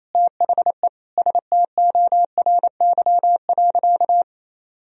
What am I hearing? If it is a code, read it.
Morse code, heard as a keyed tone, about 27 wpm, 700 Hz. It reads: THE STORY.
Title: Sentences from Top 1000 Words - Review Part 2 - 28wpm